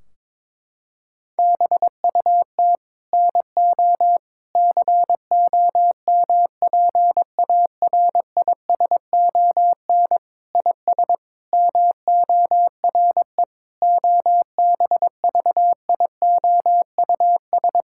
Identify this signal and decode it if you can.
Morse code, heard as a keyed tone, about 22 wpm, 700 Hz. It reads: BUT NO COMPARISON IS MORE OBVIOUS
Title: Sentences from Top 1600 Words 22wpm